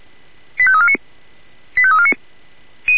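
Five-tone selective call (ZVEI), heard as quick runs of five tones, tones of about 70 ms. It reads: 85349 85349
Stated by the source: A ZVEI five-tone selective call received by SDR.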